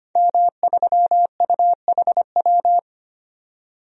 Morse code, heard as a keyed tone, about 25 wpm, 700 Hz. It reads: M3UHW
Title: DXCC-100 Call Signs - 5 Characters - 25wpm